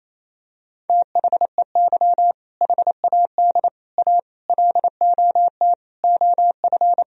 Morse code, heard as a keyed tone, about 28 wpm, 700 Hz. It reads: THEY HAD A LOT OF
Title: Sentences from Top 1000 Words - Review Part 3 - 28wpm